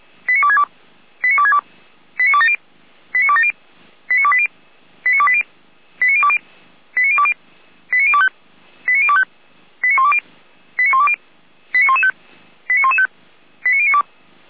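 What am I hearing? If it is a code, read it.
Five-tone selective call (ZVEI), heard as quick runs of five tones, tones of about 70 ms. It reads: 89262 89262 89270 89270 89280 89280 89020 89020 89025 89025 89120 89120 89115 89115 89092
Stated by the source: ZVEI five-tone call received by SDR.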